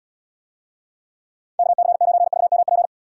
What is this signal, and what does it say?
Morse code, heard as a keyed tone, about 78 wpm, 690 Hz. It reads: KC9VMP